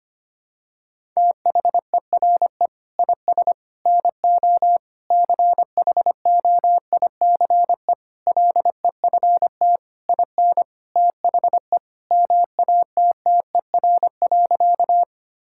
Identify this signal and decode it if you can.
Morse code, heard as a keyed tone, about 25 wpm, 700 Hz. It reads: THERE IS NO CHOICE LEFT IN THE MATTER.
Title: Sentences from Top 1000 Words - Review Part 6 - 25wpm